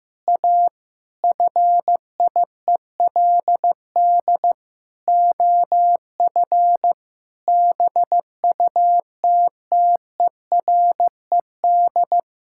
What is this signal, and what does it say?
Morse code, heard as a keyed tone, about 15 wpm, 700 Hz. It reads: A FIELD OF BUTTERED
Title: Pangram Challenge v2 - 15wpm